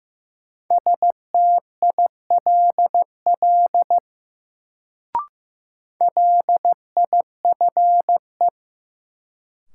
Morse code, heard as a keyed tone, about 15 wpm, 700 Hz. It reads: STILL LIFE
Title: Sentences from Top 300 Words - New Word Drill - 15wpm